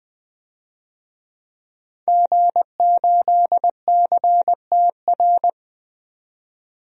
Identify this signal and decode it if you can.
Morse code, heard as a keyed tone, about 20 wpm, 700 Hz. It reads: G8CTR
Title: DXCC-100 Call Signs - 5 Characters - 20wpm